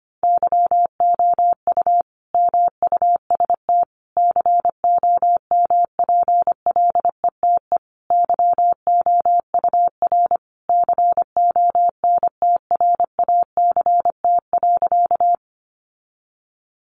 Morse code, heard as a keyed tone, about 25 wpm, 700 Hz. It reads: YOU MUST COMPLETE YOUR CONTRACT.
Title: Sentences from Top 1400 Words 25wpm